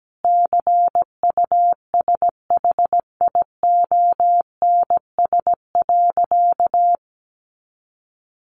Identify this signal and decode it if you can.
Morse code, heard as a keyed tone, about 17 wpm, 700 Hz. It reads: CUSHIONS.
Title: A Sound Of Thunder 17wpm